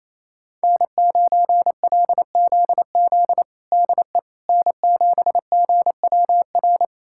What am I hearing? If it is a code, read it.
Morse code, heard as a keyed tone, about 28 wpm, 700 Hz. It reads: N9LZZ DE N7GWR
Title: USA Callsign DE Callsign - 5 Characters - 28wpm